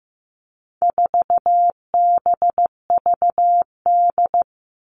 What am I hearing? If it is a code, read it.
Morse code, heard as a keyed tone, about 14 wpm, 700 Hz. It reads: 4BVD
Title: All Cut Numbers - 4-Digit 15wpm